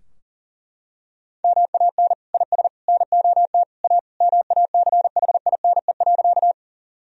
Morse code, heard as a keyed tone, about 40 wpm, 700 Hz. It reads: MAN IS NOT A MACHINE.
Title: Sentences from Top 1000 Words - Review Part 2 - 40wpm